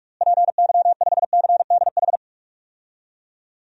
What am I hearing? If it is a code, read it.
Morse code, heard as a keyed tone, about 46 wpm, 700 Hz. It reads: PY5CDH
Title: DXCC-100 Call Signs - 6 Characters - Repeated 3x - 45wpm